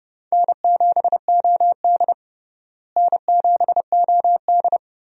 Morse code, heard as a keyed tone, about 29 wpm, 700 Hz. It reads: N7OD N7OD
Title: USA Call Signs - 4 Characters - Repeated 3x - 30wpm